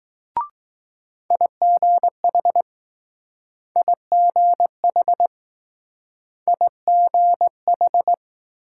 Morse code, heard as a keyed tone, about 20 wpm, 700 Hz. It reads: IGH IGH IGH